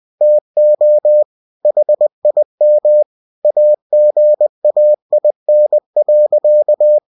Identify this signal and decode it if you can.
Morse code, heard as a keyed tone, about 20 wpm, 600 Hz. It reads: TO HIM AGAIN.
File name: cw-books / winnie-the-pooh / winnie-20wpm-page-0011